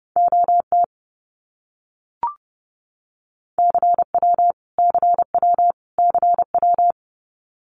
Morse code, heard as a keyed tone, about 30 wpm, 700 Hz. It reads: OT CW CW CW
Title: Abbreviations Repeated 3x - 30wpm